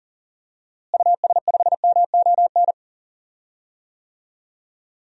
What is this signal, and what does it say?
Morse code, heard as a keyed tone, about 40 wpm, 700 Hz. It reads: US5MON
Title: DXCC-100 Call Signs - 6 Characters - Repeated 3x - 40wpm